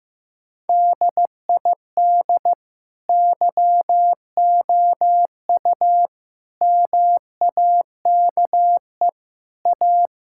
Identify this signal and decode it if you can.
Morse code, heard as a keyed tone, about 15 wpm, 700 Hz. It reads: DID YOU MAKE A
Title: Sentences from Top 1000 Words - Review Part 6 - 15wpm